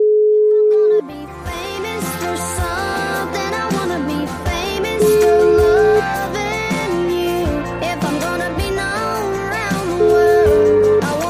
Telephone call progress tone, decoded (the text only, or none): ringback tone